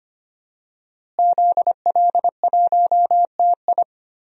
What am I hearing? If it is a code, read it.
Morse code, heard as a keyed tone, about 25 wpm, 700 Hz. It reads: ZL1TI
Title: DXCC-100 Call Signs - 5 Characters - Repeated 2x - 25wpm